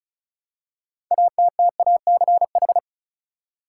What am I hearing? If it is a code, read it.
Morse code, heard as a keyed tone, about 36 wpm, 700 Hz. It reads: ATTACH
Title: Six Letter Words - 35wpm